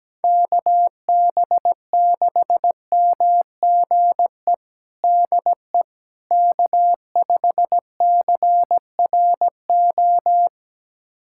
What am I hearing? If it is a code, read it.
Morse code, heard as a keyed tone, about 17 wpm, 700 Hz. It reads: KB6MGE DE K5CRO
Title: USA Callsign DE Callsign 17wpm